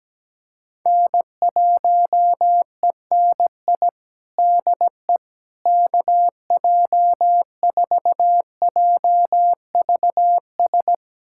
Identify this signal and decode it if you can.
Morse code, heard as a keyed tone, about 17 wpm, 700 Hz. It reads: N1ENI DE KJ4JVS
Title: USA Callsign DE Callsign 17wpm